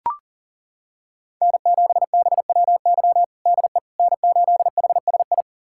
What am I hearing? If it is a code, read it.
Morse code, heard as a keyed tone, about 40 wpm, 690 Hz. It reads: N7BWY DE N8HSI
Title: USA Callsign DE Callsign - 5 Characters - 40wpm